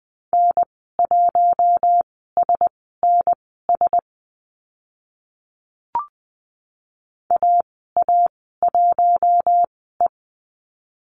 Morse code, heard as a keyed tone, about 20 wpm, spaced as for 15 wpm, 700 Hz. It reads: N1SNS AA1E
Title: 042 Lesson 8g - Number 1 - Call Signs 20wpm